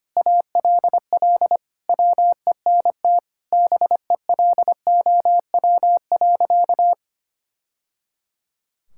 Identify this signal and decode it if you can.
Morse code, heard as a keyed tone, about 25 wpm, 700 Hz. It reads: ALL WENT BELOW.
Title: Sentences from Top 1000 Words - Review Part 5 - 25wpm